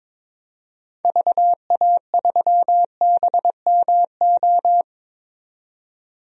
Morse code, heard as a keyed tone, about 22 wpm, 700 Hz. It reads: VA3BMO
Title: DXCC-100 Call Signs - 6 Characters - 22wpm